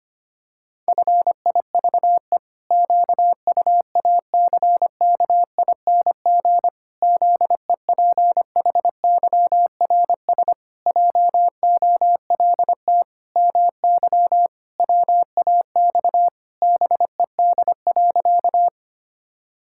Morse code, heard as a keyed tone, about 25 wpm, 700 Hz. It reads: FIVE QUACKING ZEPHYRS JOLT MY WAX BED.
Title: Pangram Challenge v2 - 25wpm